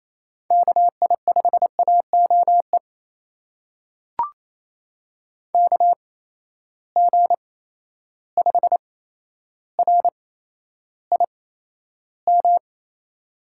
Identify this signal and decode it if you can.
Morse code, heard as a keyed tone, about 28 wpm, spaced as for 12 wpm, 700 Hz. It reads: KI5AOE KG5RIM
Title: USA Call Signs - 6 Characters - 8x Character Spacing - 28wpm